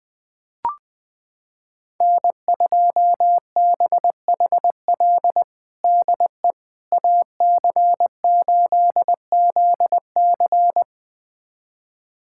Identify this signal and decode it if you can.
Morse code, heard as a keyed tone, about 20 wpm, 700 Hz. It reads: N2BHL DE AC8ZC
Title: USA Callsign DE Callsign - 5 Characters - 20wpm